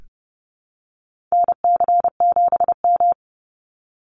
Morse code, heard as a keyed tone, about 30 wpm, 700 Hz. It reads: NC7M
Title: Call Signs Repeated 3x - 30wpm